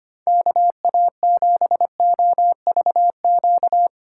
Morse code, heard as a keyed tone, about 25 wpm, 700 Hz. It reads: KA7OVQ
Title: USA Call Signs - 6 Characters - 4x Character Spacing - 25wpm